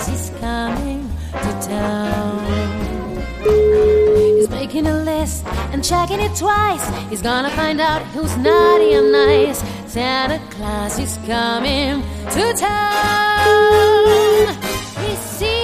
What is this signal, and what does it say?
Ringback tone (Europe, 425 Hz), with speech.